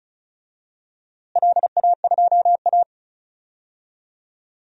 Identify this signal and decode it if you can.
Morse code, heard as a keyed tone, about 34 wpm, 700 Hz. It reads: LA2A
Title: DXCC-100 Call Signs - 4 Characters - Repeated 2x - 35wpm